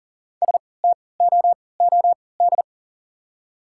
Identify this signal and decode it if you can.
Morse code, heard as a keyed tone, about 40 wpm, spaced as for 23 wpm, 690 Hz. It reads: STOOD
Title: 3x Character Spacing - Top 500 Words - 40wpm